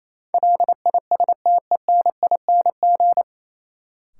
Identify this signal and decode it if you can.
Morse code, heard as a keyed tone, about 28 wpm, 700 Hz. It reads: LISTENING
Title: Sets of 5 Words - New Word Drill - 28wpm